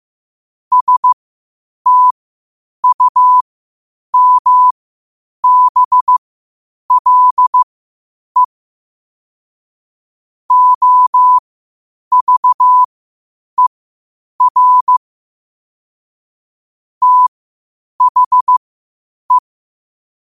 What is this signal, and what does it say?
Morse code, heard as a keyed tone, about 15 wpm, spaced as for 8 wpm, 1000 Hz. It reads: STUMBLE OVER THE
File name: qotdfiles / 8wpm